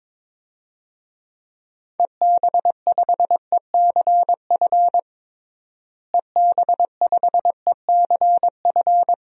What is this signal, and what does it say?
Morse code, heard as a keyed tone, about 22 wpm, 700 Hz. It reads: EB5ECF EB5ECF
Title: DXCC-100 Call Signs - 6 Characters - Repeated 2x - 22wpm